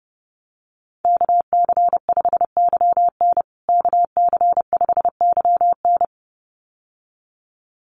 Morse code, heard as a keyed tone, about 30 wpm, 700 Hz. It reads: KC5YN KC5YN
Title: USA Call Signs Repeated 2x 30wpm